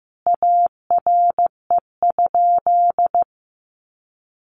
Morse code, heard as a keyed tone, about 15 wpm, 700 Hz. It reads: ARE?
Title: Sentences from Top 1400 Words 15wpm